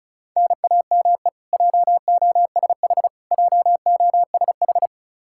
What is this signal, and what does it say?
Morse code, heard as a keyed tone, about 34 wpm, 700 Hz. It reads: NAME JOSH JOSH
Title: Practice QSO - Exchange 1 - Beginning - 35wpm